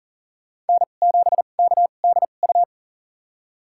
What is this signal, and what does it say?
Morse code, heard as a keyed tone, about 40 wpm, spaced as for 29 wpm, 700 Hz. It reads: N7KDU
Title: USA Call Signs - 5 Characters - 2x Character Spacing - 40wpm